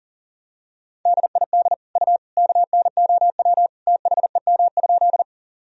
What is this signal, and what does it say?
Morse code, heard as a keyed tone, about 37 wpm, 700 Hz. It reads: DID U KNOW THEM?
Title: Sentences from Top 100 Words and Abbreviations 40wpm